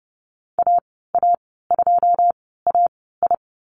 Morse code, heard as a keyed tone, about 30 wpm, spaced as for 17 wpm, 700 Hz. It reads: AA2AI
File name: Lesson 14g - Number 2 - Call Signs 30wpm